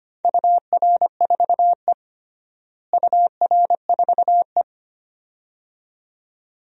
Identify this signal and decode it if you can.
Morse code, heard as a keyed tone, about 25 wpm, 700 Hz. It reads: UR4E UR4E
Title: DXCC-100 Call Signs - 4 Characters - Repeated 2x - 25wpm